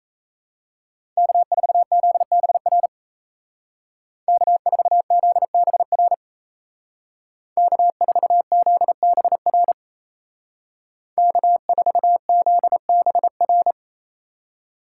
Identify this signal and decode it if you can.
Morse code, heard as a keyed tone, about 37 wpm, 700 Hz. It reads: K4ZBR K4ZBR K4ZBR K4ZBR